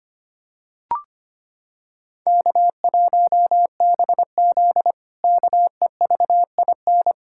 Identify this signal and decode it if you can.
Morse code, heard as a keyed tone, about 25 wpm, 700 Hz. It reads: K1BZ KEVIN
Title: K1USN SST QSO Practice - 25wpm